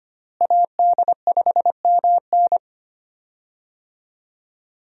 Morse code, heard as a keyed tone, about 25 wpm, 700 Hz. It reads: AD5MN
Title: USA Call Signs - 5 Characters - Repeated 3x - 25wpm